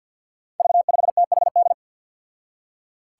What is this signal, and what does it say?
Morse code, heard as a keyed tone, about 51 wpm, 700 Hz. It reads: V5THD
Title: Five Challenging Letter-Number Combinations 50wpm